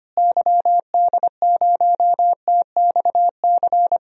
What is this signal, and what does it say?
Morse code, heard as a keyed tone, about 25 wpm, 700 Hz. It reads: YD0TXC
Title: DXCC-100 Call Signs - 6 Characters - Repeated 3x - 25wpm